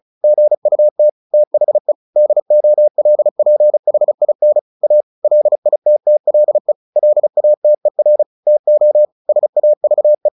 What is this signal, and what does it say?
Morse code, heard as a keyed tone, about 34 wpm, 600 Hz. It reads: GUT THE DOLPHIN A LITTLE LATER TO SAVE
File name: the old man and the sea 2 35wpm